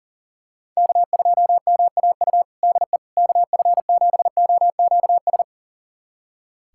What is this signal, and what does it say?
Morse code, heard as a keyed tone, about 37 wpm, 700 Hz. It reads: K2MAU DE KF7OQS
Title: USA Callsign DE Callsign 40wpm